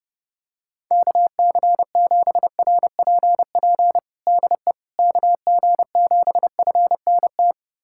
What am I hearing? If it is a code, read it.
Morse code, heard as a keyed tone, about 30 wpm, 700 Hz. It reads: KC7RPP DE KG7FNT
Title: USA Callsign DE Callsign - 6 Characters - 30wpm